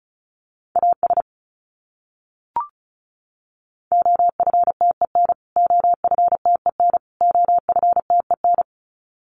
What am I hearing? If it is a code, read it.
Morse code, heard as a keyed tone, about 34 wpm, 700 Hz. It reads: AS OFTEN OFTEN OFTEN
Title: Top 500 Words Repeated 35wpm